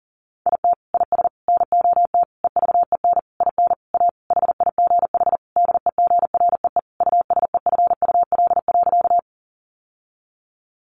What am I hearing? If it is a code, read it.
Morse code, heard as a keyed tone, about 40 wpm, 700 Hz. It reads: IT IS NOT EVEN IN A HIGH DEGREE USEFUL.
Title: Sentences from Top 1500 Words 40wpm